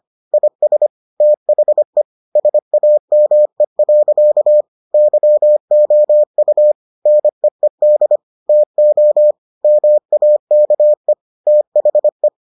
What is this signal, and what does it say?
Morse code, heard as a keyed tone, about 25 wpm, 600 Hz. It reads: IS THE SAME. YOU NEED TO MAKE THE